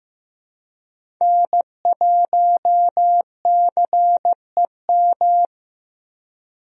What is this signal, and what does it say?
Morse code, heard as a keyed tone, about 15 wpm, 700 Hz. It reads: N1CEM
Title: USA Call Signs - 5 Characters - Repeated 2x - 15wpm